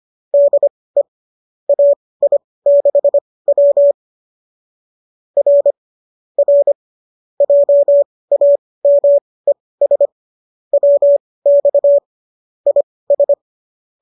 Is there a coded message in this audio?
Yes: Morse code from 0.3 to 13.3 s, heard as a keyed tone, about 25 wpm, spaced as for 18 wpm, 590 Hz. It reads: DE AI6W R R JAMES WX IS